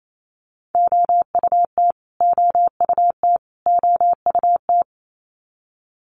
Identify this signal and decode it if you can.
Morse code, heard as a keyed tone, about 28 wpm, 700 Hz. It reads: OUT OUT OUT
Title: Top 500 Words Repeated 28wpm